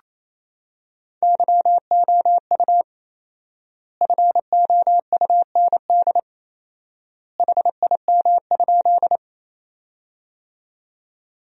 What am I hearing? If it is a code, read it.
Morse code, heard as a keyed tone, about 28 wpm, 700 Hz. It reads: YOU FOUND HIM?
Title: 4x Word-Spacing - Sentences from Top 300 Words - 28wpm